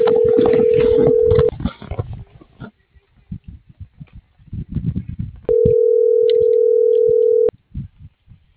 Ringback tone (North America), 440 + 480 Hz.